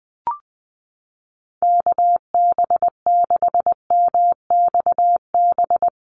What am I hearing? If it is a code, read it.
Morse code, heard as a keyed tone, about 20 wpm, 700 Hz. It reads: KB6MXB
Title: USA Call Signs Repeated 2x 20wpm